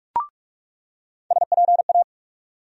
Morse code, heard as a keyed tone, about 43 wpm, 700 Hz. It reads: SPA
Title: USA Call Sign Postfixes - 3 Letters 45wpm